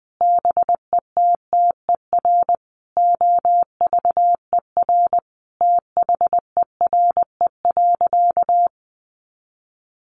Morse code, heard as a keyed tone, about 20 wpm, 700 Hz. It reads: BETTER OVER THERE.
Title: Sentences from Top 1500 Words 20wpm